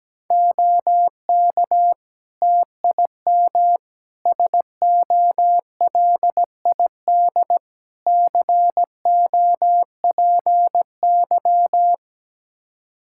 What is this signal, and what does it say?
Morse code, heard as a keyed tone, about 17 wpm, 700 Hz. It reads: OK TIM SOLID COPY
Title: Practice QSO - Exchange 3 - More Info - 17wpm